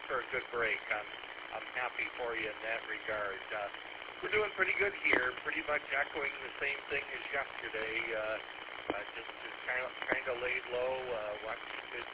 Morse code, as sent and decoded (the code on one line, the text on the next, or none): none